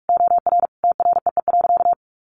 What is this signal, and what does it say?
Morse code, heard as a keyed tone, about 44 wpm, 700 Hz. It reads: OR TREE.
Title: A Sound Of Thunder 45wpm